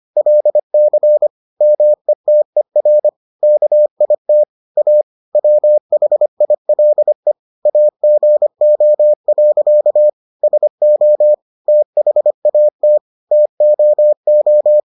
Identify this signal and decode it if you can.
Morse code, heard as a keyed tone, about 25 wpm, 600 Hz. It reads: LC METER KIT A WHILE AGO. SO THAT TOO